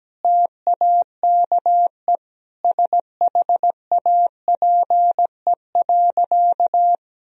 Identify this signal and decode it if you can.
Morse code, heard as a keyed tone, about 17 wpm, 700 Hz. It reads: TAKE SHAPE.
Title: Sentences from Top 1000 Words - Review Part 3 - 17wpm